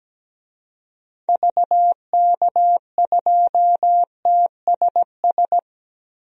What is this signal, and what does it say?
Morse code, heard as a keyed tone, about 17 wpm, 700 Hz. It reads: VK2TSS
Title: DXCC-100 Call Signs - 6 Characters - Repeated 2x - 17wpm